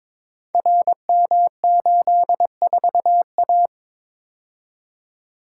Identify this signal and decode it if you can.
Morse code, heard as a keyed tone, about 22 wpm, 700 Hz. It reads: RM84A
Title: Code Groups - 5 Letter:Numbers - 22wpm